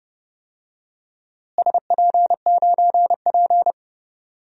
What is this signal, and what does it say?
Morse code, heard as a keyed tone, about 30 wpm, 700 Hz. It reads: SP9P
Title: DXCC-100 Call Signs - 4 Characters - Repeated 2x - 30wpm